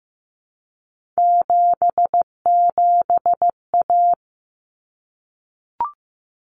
Morse code, heard as a keyed tone, about 15 wpm, 700 Hz. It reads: 77A
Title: Common Cut Numbers - 3-Digit 15wpm